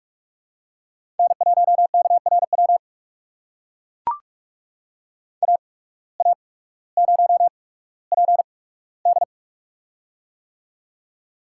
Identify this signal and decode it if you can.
Morse code, heard as a keyed tone, about 44 wpm, spaced as for 19 wpm, 690 Hz. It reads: N1KRW AA0PD
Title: USA Call Signs - 5 Characters - 8x Character Spacing - 45wpm